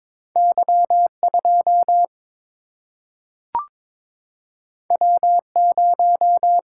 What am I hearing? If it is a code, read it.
Morse code, heard as a keyed tone, about 22 wpm, 700 Hz. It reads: Y2 W0
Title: Single-Letter Single-Number 22wpm